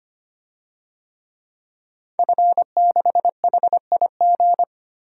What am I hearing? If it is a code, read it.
Morse code, heard as a keyed tone, about 25 wpm, 700 Hz. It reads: F6HIG
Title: DXCC-100 Call Signs - 5 Characters - 25wpm